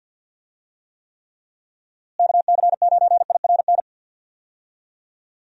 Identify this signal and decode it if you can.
Morse code, heard as a keyed tone, about 50 wpm, 700 Hz. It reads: KC9IRN
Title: USA Call Signs - 6 Characters - Repeated 3x - 50wpm